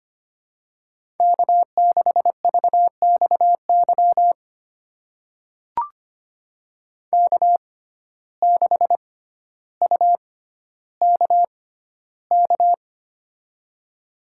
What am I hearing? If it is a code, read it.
Morse code, heard as a keyed tone, about 25 wpm, spaced as for 13 wpm, 700 Hz. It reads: K6VXY K6UKK
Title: USA Call Signs - 5 Characters - 6x Character Spacing - 25wpm